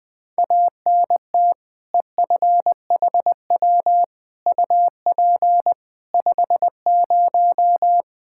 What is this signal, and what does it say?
Morse code, heard as a keyed tone, about 21 wpm, 700 Hz. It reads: ANT EFHW UP 50
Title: Practice QSO - Exchange 2 - Info - 20wpm